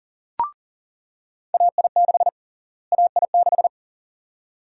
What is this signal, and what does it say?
Morse code, heard as a keyed tone, about 40 wpm, 700 Hz. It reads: AI6 AI6
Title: USA Call Sign Prefixes - Repeated 2x - 40wpm